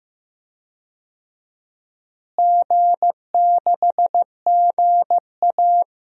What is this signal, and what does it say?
Morse code, heard as a keyed tone, about 15 wpm, 700 Hz. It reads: G6GA
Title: DXCC-100 Call Signs - 4 Characters - Repeated 2x - 15wpm